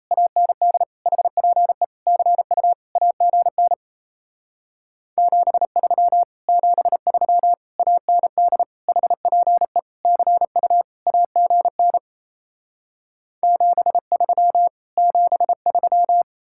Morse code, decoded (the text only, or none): AND HPE CU AGN 73 73 AND HPE CU AGN 73 73